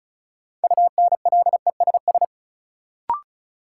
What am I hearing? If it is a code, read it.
Morse code, heard as a keyed tone, about 36 wpm, 700 Hz. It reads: UNLESS
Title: Sentences from Top 1000 Words - Review Part 5 - New Word Drill - 35wpm